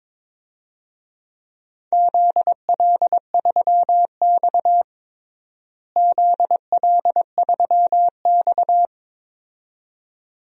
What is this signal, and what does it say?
Morse code, heard as a keyed tone, about 21 wpm, 700 Hz. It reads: ZL3X ZL3X
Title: DXCC-100 Call Signs - 4 Characters - Repeated 2x - 22wpm